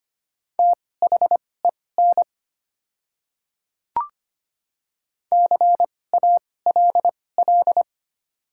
Morse code, heard as a keyed tone, about 25 wpm, spaced as for 18 wpm, 700 Hz. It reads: THEN CALL
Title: 2x Character Spacing - Top 500 Words - 25wpm